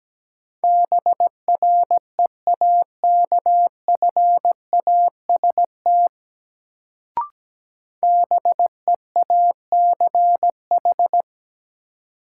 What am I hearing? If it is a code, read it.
Morse code, heard as a keyed tone, about 17 wpm, 700 Hz. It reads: BREAKFAST BEACH
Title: Sentences from Top 1500 Words - New Word Drill Alt - 17wpm